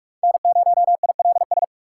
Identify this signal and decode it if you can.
Morse code, heard as a keyed tone, about 44 wpm, 690 Hz. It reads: N0ILS
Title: USA Call Signs - 5 Characters - 6x Character Spacing - 45wpm